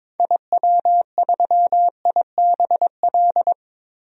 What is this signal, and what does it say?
Morse code, heard as a keyed tone, about 22 wpm, 700 Hz. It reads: IW3IBL
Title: DXCC-100 Call Signs - 6 Characters - Repeated 3x - 22wpm